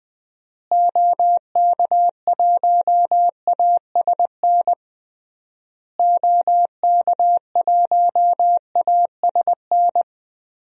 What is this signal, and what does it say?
Morse code, heard as a keyed tone, about 20 wpm, 700 Hz. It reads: OK1ASN OK1ASN
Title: DXCC-100 Call Signs - 6 Characters - Repeated 3x - 20wpm